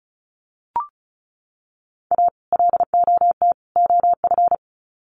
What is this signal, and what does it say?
Morse code, heard as a keyed tone, about 34 wpm, 700 Hz. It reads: A LOT OF
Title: Sets of 3 Words 35wpm